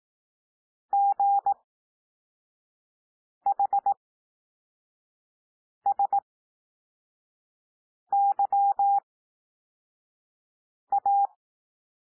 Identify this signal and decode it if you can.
Morse code, heard as a keyed tone, about 18 wpm, spaced as for 4 wpm, 800 Hz. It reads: GHSYA
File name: cwmp3 / L2 SOUND LETTERS 18 wpm